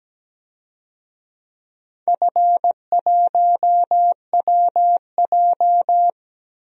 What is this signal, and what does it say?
Morse code, heard as a keyed tone, about 17 wpm, 700 Hz. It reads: F1WJ
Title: DXCC-100 Call Signs - 4 Characters - Repeated 3x - 17wpm